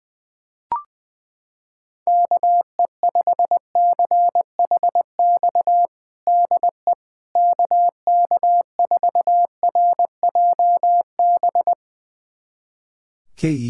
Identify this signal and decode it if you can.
Morse code, heard as a keyed tone, about 20 wpm, 700 Hz. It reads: KE5CHX DE KK4RJB
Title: USA Callsign DE Callsign - 6 Characters - 20wpm